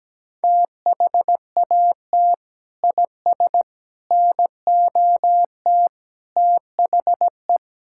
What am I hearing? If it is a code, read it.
Morse code, heard as a keyed tone, about 17 wpm, 700 Hz. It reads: THAT IS NOT THE
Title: Sentences from Top 1000 Words - Review Part 5 - 17wpm